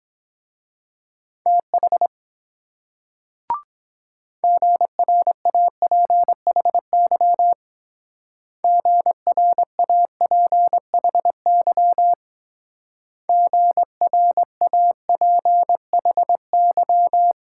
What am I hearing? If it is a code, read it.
Morse code, heard as a keyed tone, about 22 wpm, 700 Hz. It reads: TH GRAPHY GRAPHY GRAPHY